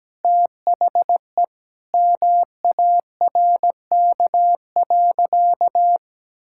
Morse code, heard as a keyed tone, about 17 wpm, 700 Hz. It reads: THE MARK.
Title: Sentences from Top 1000 Words - Review Part 6 - 17wpm